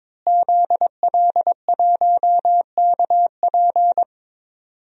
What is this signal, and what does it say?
Morse code, heard as a keyed tone, about 22 wpm, 700 Hz. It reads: ZL1KP
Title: DXCC-100 Call Signs - 5 Characters - Repeated 3x - 22wpm